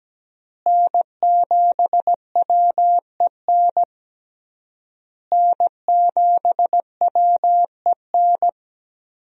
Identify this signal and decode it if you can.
Morse code, heard as a keyed tone, about 17 wpm, 700 Hz. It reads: N7WEN N7WEN
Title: USA Call Signs - 5 Characters - Repeated 3x - 17wpm